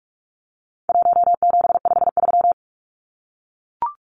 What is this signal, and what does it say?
Morse code, heard as a keyed tone, about 44 wpm, 700 Hz. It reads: 1753
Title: Numbers - 4-Digit 45wpm